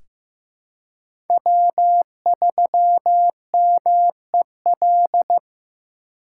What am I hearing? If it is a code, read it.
Morse code, heard as a keyed tone, about 15 wpm, 700 Hz. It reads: W3MEL
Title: K1USN SST QSO Practice - 15wpm